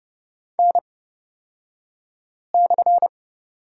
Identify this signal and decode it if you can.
Morse code, heard as a keyed tone, about 31 wpm, spaced as for 5 wpm, 700 Hz. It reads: N/
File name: Lesson 37d - Letter x - Two Characters 30wpm